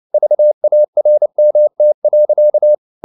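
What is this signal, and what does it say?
Morse code, heard as a keyed tone, about 29 wpm, 600 Hz. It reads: VARMT.
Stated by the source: Fast (QRQ) Morse at 29 wpm, from CW med Gnister.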